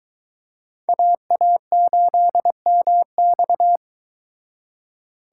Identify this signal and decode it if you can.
Morse code, heard as a keyed tone, about 23 wpm, 700 Hz. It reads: AA8MX